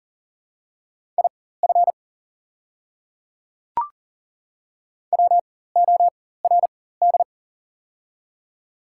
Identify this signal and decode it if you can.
Morse code, heard as a keyed tone, about 37 wpm, spaced as for 18 wpm, 700 Hz. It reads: IF WORD
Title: 4x Character Spacing - Top 200 Words - 40wpm